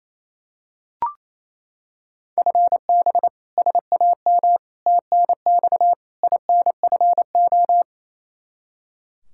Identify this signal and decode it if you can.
Morse code, heard as a keyed tone, about 28 wpm, 700 Hz. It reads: FB SAM TNX INFO
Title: Practice QSO - Exchange 3 - More Info - 28wpm